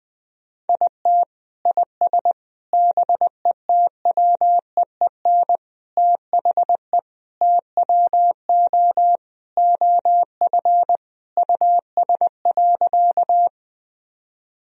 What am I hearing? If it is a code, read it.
Morse code, heard as a keyed tone, about 20 wpm, 700 Hz. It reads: IT IS BETWEEN THE TWO OF US.
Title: Sentences from Top 1000 Words - Review Part 1 - 20wpm